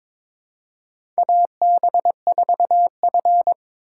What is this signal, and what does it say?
Morse code, heard as a keyed tone, about 23 wpm, 700 Hz. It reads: AB4F
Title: USA Call Signs - 4 Characters - 4x Character Spacing - 22wpm